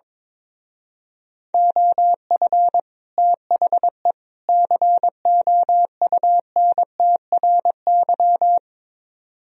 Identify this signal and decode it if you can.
Morse code, heard as a keyed tone, about 22 wpm, 700 Hz. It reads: OF THE COUNTRY
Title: Speed-Racing - Sets of 3 Words 22wpm